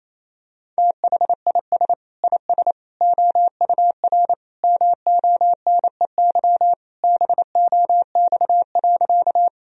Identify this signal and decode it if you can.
Morse code, heard as a keyed tone, about 28 wpm, 700 Hz. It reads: THIS IS OUR MONEY BOX.
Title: Sentences from Top 1000 Words - Review Part 2 - 28wpm